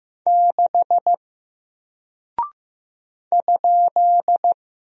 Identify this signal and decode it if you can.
Morse code, heard as a keyed tone, about 15 wpm, 700 Hz. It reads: 6 ?
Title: Single Character (Letter:Number:Punctuation:Prosign) Alt 15wpm